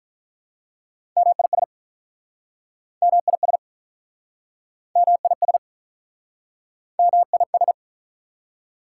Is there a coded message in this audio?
Yes: Morse code from 1.2 to 7.7 s, heard as a keyed tone, about 47 wpm, 690 Hz. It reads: MIS MIS MIS MIS